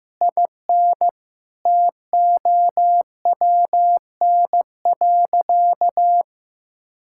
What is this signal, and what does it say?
Morse code, heard as a keyed tone, about 16 wpm, 700 Hz. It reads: IN TOWN.
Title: Sentences from Top 1600 Words 15wpm